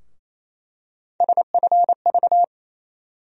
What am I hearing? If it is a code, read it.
Morse code, heard as a keyed tone, about 28 wpm, 700 Hz. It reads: SFV
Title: USA Call Sign Postfixes - 3 Letters 28wpm